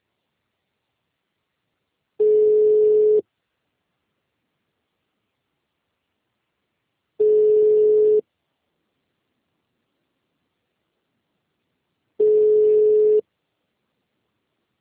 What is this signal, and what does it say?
Ringback tone (Europe, 425 Hz).